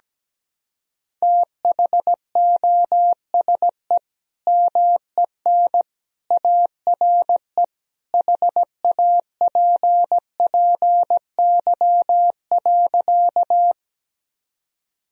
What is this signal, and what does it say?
Morse code, heard as a keyed tone, about 17 wpm, 700 Hz. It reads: THOSE MEN ARE HAPPY.
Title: Sentences from Top 1000 Words - Review Part 1 - 17wpm